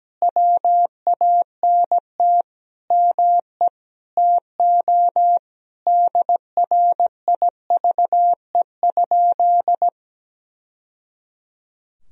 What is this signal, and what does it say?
Morse code, heard as a keyed tone, about 17 wpm, 700 Hz. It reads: WANT ME TO DRIVE?
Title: Sentences from Top 1000 Words - Review Part 2 - 17wpm